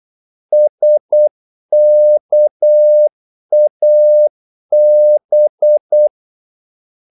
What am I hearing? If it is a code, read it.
Morse code, heard as a keyed tone, about 8 wpm, 600 Hz. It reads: SKAB
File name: Korte ord DK 8wpm